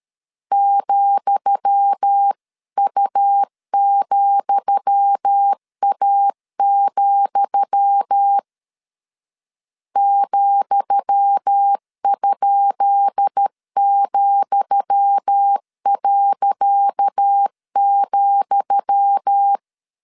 Morse code, heard as a keyed tone, about 13 wpm, 800 Hz. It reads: ,U,A, ,?,.,
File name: LEZIONI / 10 SIGNS / 10D COMA 12wpm 2544